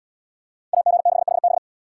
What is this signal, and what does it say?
Morse code, heard as a keyed tone, about 70 wpm, 700 Hz. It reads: AG6HB